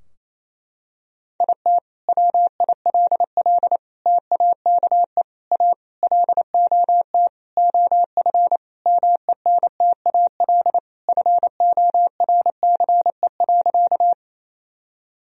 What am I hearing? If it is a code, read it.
Morse code, heard as a keyed tone, about 28 wpm, 700 Hz. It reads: IT WILL TAKE A LOT OF MENTAL FORCE.
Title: Sentences from Top 1000 Words - Review Part 4 - 28wpm